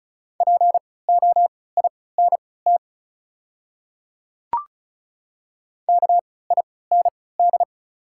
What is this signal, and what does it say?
Morse code, heard as a keyed tone, about 34 wpm, spaced as for 20 wpm, 700 Hz. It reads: POINT KIND
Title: 3x Character Spacing - Top 300 Words - 35wpm